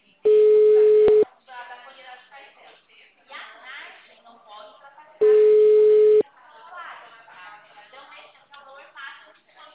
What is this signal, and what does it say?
Ringback tone (Europe, 425 Hz), with speech.
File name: out- -2003- - - .972